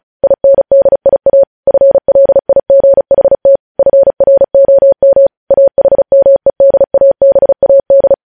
Morse code, heard as a keyed tone, about 36 wpm, 550 Hz. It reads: INDIA FLIGHT FROM AHMEDABAD
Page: news headlines in morse code at 35 wpm